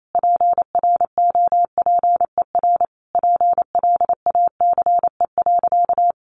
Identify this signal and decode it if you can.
Morse code, heard as a keyed tone, about 28 wpm, 700 Hz. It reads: PROPER PLACE.
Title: Sentences from Top 800 Words 28wpm